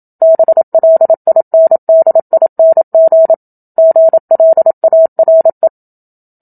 Morse code, heard as a keyed tone, about 27 wpm, 650 Hz. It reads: BLINDING GLARE
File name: Book Two / 01-Chapter 1 - 27 WPM